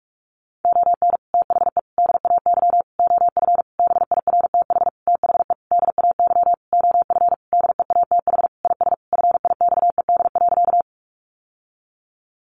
Morse code, heard as a keyed tone, about 44 wpm, 700 Hz. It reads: ON THE DAY OF BIRTH THE DAY OF DEATH IS FIXED.
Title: Sentences from Top 1100 Words 45wpm